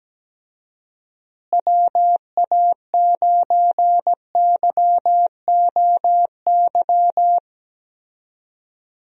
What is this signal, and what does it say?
Morse code, heard as a keyed tone, about 17 wpm, 700 Hz. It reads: WA9YOY
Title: USA Call Signs - 6 Characters - Repeated 2x - 17wpm